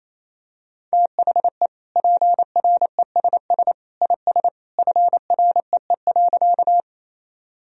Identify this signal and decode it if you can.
Morse code, heard as a keyed tone, about 28 wpm, 700 Hz. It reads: THE PRESS IS FREE.
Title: Sentences from Top 1000 Words - Review Part 1 - 28wpm